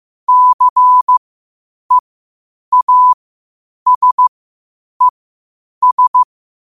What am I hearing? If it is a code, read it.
Morse code, heard as a keyed tone, about 15 wpm, spaced as for 8 wpm, 1000 Hz. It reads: CEASES
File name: qotdfiles / 8wpm